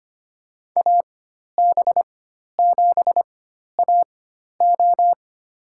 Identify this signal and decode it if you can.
Morse code, heard as a keyed tone, about 25 wpm, spaced as for 12 wpm, 700 Hz. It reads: AB7AO
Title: USA Call Signs - 5 Characters - 4x Character Spacing - 25wpm